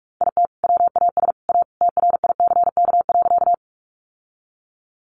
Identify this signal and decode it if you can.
Morse code, heard as a keyed tone, about 44 wpm, 700 Hz. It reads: IT WAS A TRICK.
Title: Sentences from Top 1500 Words 45wpm